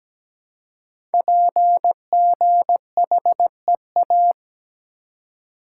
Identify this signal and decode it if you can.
Morse code, heard as a keyed tone, about 17 wpm, 700 Hz. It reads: PGHEA
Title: Code Groups - 5 Letters - 17wpm